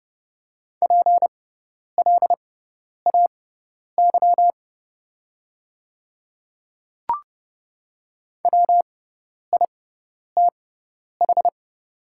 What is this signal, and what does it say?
Morse code, heard as a keyed tone, about 30 wpm, spaced as for 10 wpm, 700 Hz. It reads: PLAY WITH
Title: 6x Character Spacing - Top 200 Words - 30wpm